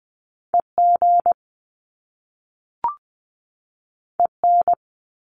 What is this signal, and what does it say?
Morse code, heard as a keyed tone, about 20 wpm, 700 Hz. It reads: EG EN
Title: Top Two Letter Combinations in Words 20wpm